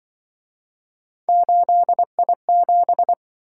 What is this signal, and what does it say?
Morse code, heard as a keyed tone, about 24 wpm, 700 Hz. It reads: 8I7
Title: Speed-Racing - Three Letter-Number Combinations - 20wpm